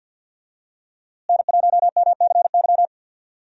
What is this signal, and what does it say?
Morse code, heard as a keyed tone, about 46 wpm, 700 Hz. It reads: N1MCY
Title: USA Call Signs - 5 Characters - Repeated 3x - 50wpm